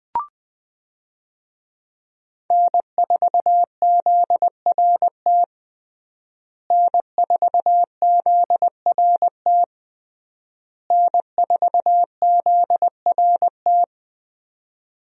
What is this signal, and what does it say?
Morse code, heard as a keyed tone, about 20 wpm, 700 Hz. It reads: N4ZRT N4ZRT N4ZRT
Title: USA Call Signs - 5 Characters - Repeated 3x - 20wpm